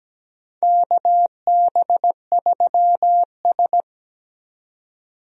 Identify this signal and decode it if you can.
Morse code, heard as a keyed tone, about 17 wpm, 700 Hz. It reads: KB3S
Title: USA Call Signs - 4 Characters - 17wpm